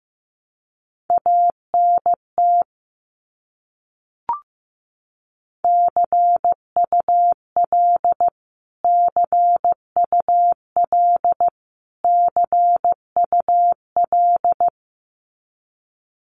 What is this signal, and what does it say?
Morse code, heard as a keyed tone, about 15 wpm, 700 Hz. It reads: ANT CUL CUL CUL
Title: Abbreviations Repeated 3x - 15wpm